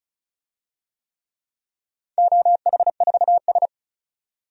Morse code, heard as a keyed tone, about 34 wpm, 700 Hz. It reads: OH4S
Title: DXCC-100 Call Signs - 4 Characters - 35wpm